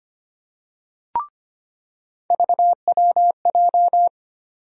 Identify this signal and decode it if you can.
Morse code, heard as a keyed tone, about 25 wpm, 700 Hz. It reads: VWJ
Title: USA Call Sign Postfixes - 3 Letters 25wpm